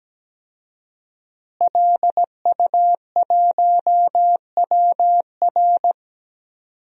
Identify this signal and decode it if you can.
Morse code, heard as a keyed tone, about 17 wpm, 700 Hz. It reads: LU1WR
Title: DXCC-100 Call Signs - 5 Characters - Repeated 3x - 17wpm